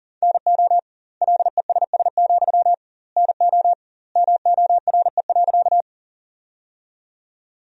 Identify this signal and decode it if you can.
Morse code, heard as a keyed tone, about 40 wpm, 700 Hz. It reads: NO LESS, NO MORE.
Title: 2x Word-Spacing - Sentences from Top 400 Words - 40wpm